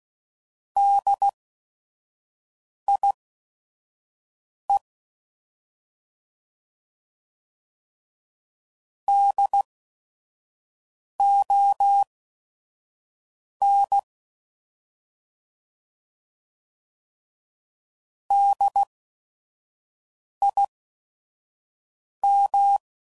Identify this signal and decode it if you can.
Morse code, heard as a keyed tone, about 16 wpm, spaced as for 5 wpm, 800 Hz. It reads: DIE DON DIM
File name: Prefixes / By First Letter / Ds